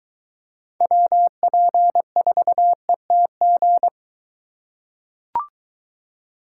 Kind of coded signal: Morse code

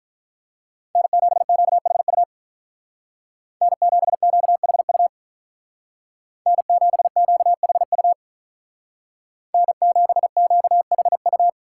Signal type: Morse code

